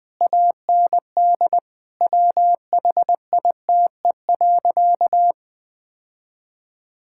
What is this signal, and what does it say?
Morse code, heard as a keyed tone, about 21 wpm, 700 Hz. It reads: AND WHITE.
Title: Sentences with 5-Word Phrases - Spoken Beginning - 20wpm